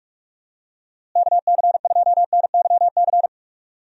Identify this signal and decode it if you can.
Morse code, heard as a keyed tone, about 42 wpm, 700 Hz. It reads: KC2NYC
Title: USA Call Signs - 6 Characters - Repeated 3x - 45wpm